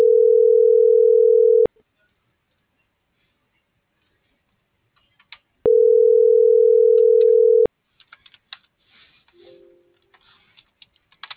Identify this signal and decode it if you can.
Ringback tone (North America), 440 + 480 Hz.